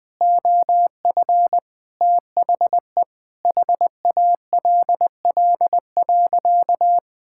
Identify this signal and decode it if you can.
Morse code, heard as a keyed tone, about 20 wpm, 700 Hz. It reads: OF THE HALL.
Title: Sentences with 5-Word Phrases - Spoken Beginning - 20wpm